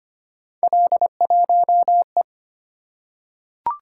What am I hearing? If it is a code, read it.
Morse code, heard as a keyed tone, about 25 wpm, 700 Hz. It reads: L1E
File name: Three Letter-Number Combinations - 25wpm